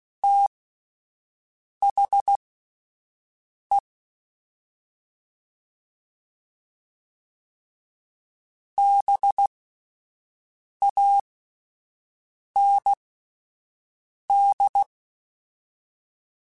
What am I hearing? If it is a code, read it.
Morse code, heard as a keyed tone, about 16 wpm, spaced as for 5 wpm, 800 Hz. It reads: THE BAND
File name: (spaced) / QSOs Spaced-mp3s / WD0BGP 1 @10 sp